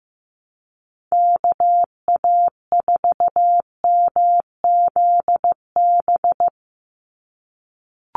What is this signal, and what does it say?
Morse code, heard as a keyed tone, about 15 wpm, 700 Hz. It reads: KA4MZB
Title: USA Call Signs 15wpm